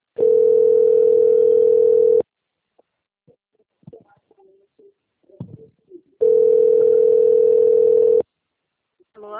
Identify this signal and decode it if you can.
Ringback tone (North America), 440 + 480 Hz, with speech.
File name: rg-1600-unknown- - - .37690